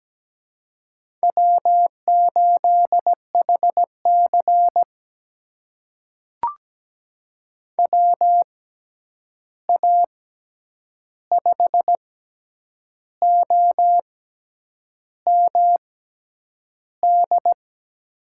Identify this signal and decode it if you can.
Morse code, heard as a keyed tone, about 17 wpm, spaced as for 8 wpm, 700 Hz. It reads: W8HC WA5OMD
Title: USA Call Signs - 6x Character Spacing - 17wpm